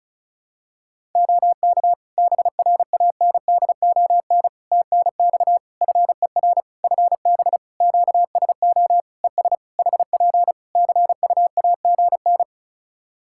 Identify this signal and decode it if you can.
Morse code, heard as a keyed tone, about 36 wpm, 700 Hz. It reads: OK BRANDON TNX FER FB QSO ES HP CUAGN
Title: K1USN SST QSO Practice 35wpm